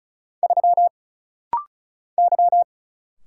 Morse code, heard as a keyed tone, about 36 wpm, 700 Hz. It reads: 3 Y
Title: Single Letter-Number - Rapid-Fire 35wpm